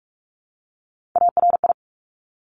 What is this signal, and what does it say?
Morse code, heard as a keyed tone, about 46 wpm, 700 Hz. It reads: ARI
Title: Top Three Letter Combinations in Words 45wpm